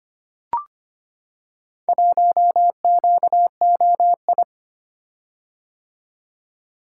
Morse code, heard as a keyed tone, about 25 wpm, 700 Hz. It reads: 1QOI
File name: practice / 25wpm / Four Letter-Number Combinations - 25wpm